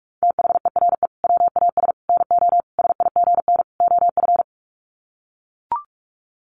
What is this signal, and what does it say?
Morse code, heard as a keyed tone, about 44 wpm, 700 Hz. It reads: THERE WAS NO SIGN OF
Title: Sets of 5 Words Encore2 45wpm